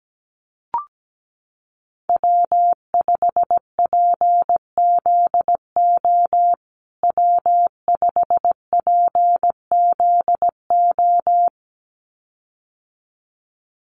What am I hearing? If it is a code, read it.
Morse code, heard as a keyed tone, about 17 wpm, 700 Hz. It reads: W5PZO W5PZO
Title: USA Call Signs Repeated 2x 17wpm